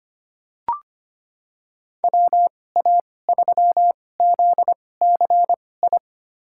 Morse code, heard as a keyed tone, about 25 wpm, spaced as for 18 wpm, 690 Hz. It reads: WA3ZCI
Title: USA Call Signs - 6 Characters - 2x Character Spacing - 25wpm